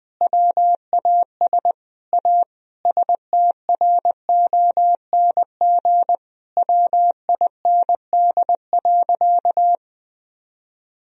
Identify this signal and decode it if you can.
Morse code, heard as a keyed tone, about 20 wpm, 700 Hz. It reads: WAS A STRONG WIND.
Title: Sentences from Top 1000 Words - Review Part 4 - 20wpm